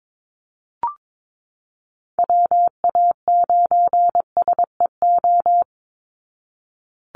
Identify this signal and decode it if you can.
Morse code, heard as a keyed tone, about 22 wpm, 700 Hz. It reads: WA9SEO
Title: Call Signs Repeated 3x - 22wpm